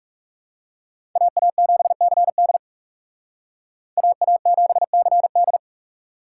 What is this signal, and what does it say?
Morse code, heard as a keyed tone, about 39 wpm, 700 Hz. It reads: AA7CD AA7CD